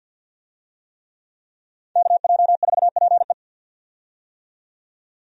Morse code, heard as a keyed tone, about 50 wpm, 700 Hz. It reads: KJ4PE
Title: USA Call Signs - 5 Characters - 50wpm